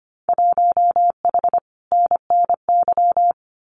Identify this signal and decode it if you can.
Morse code, heard as a keyed tone, about 25 wpm, 700 Hz. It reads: 1H NNY
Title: Winter Field Day Exchanges 25wpm